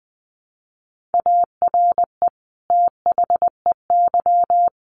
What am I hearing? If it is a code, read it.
Morse code, heard as a keyed tone, about 19 wpm, 700 Hz. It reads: ARE THEY
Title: Sentences from Top 1500 Words 20wpm